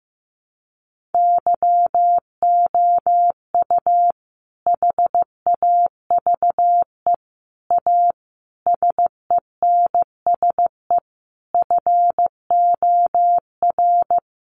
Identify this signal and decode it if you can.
Morse code, heard as a keyed tone, about 15 wpm, 700 Hz. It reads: YOU HAVE A SENSE FOR
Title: Sentences from Top 700 Words 15wpm